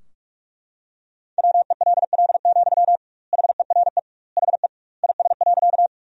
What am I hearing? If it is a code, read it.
Morse code, heard as a keyed tone, about 41 wpm, 700 Hz. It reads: WELL, HERE HE IS.
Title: 2x Word-Spacing - Sentences from Top 200 Words - 45wpm